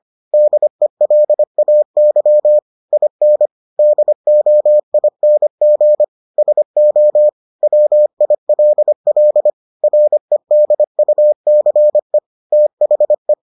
Morse code, heard as a keyed tone, about 25 wpm, 600 Hz. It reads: DELAY IN DOING SO WILL REDUCE THE